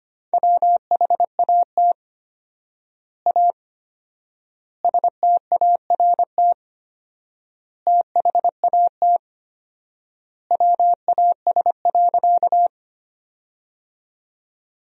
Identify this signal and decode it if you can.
Morse code, heard as a keyed tone, about 25 wpm, 700 Hz. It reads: WHAT A START THAT WAS.
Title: 4x Word-Spacing - Sentences from Top 300 Words - 25wpm